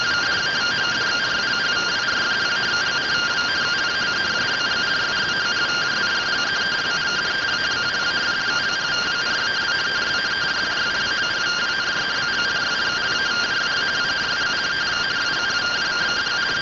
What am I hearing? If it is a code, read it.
Two-tone teleprinter signal, about 170 Hz shift.